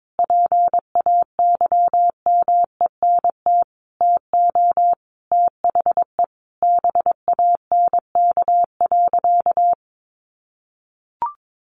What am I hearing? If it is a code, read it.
Morse code, heard as a keyed tone, about 22 wpm, 700 Hz. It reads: PAYMENT TO THE BANK.
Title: Sentences from Top 1500 Words 22wpm